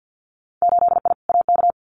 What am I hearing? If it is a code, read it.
Morse code, heard as a keyed tone, about 50 wpm, 700 Hz. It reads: 8I AK
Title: Winter Field Day Exchanges Encore 50wpm